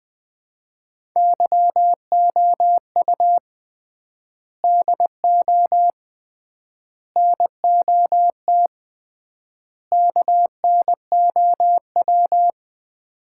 Morse code, heard as a keyed tone, about 21 wpm, 700 Hz. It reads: YOU DO NOT KNOW
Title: 3x Word-Spacing - Sets of 4 Words - 20wpm